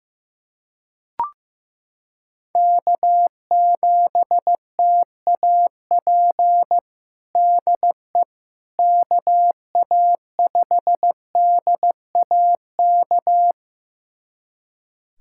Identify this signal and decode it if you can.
Morse code, heard as a keyed tone, about 15 wpm, 700 Hz. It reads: K7TAP DE KA5DAK
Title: USA Callsign DE Callsign 15wpm